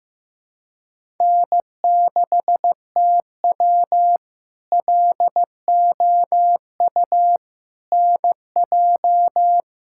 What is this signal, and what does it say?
Morse code, heard as a keyed tone, about 15 wpm, 700 Hz. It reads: N6TW LOU NJ
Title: K1USN SST QSO Practice - 15wpm